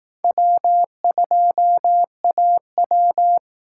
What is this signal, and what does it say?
Morse code, heard as a keyed tone, about 18 wpm, 700 Hz. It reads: W2AW
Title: Speed Racing - USA 1x2 Call Signs 15wpm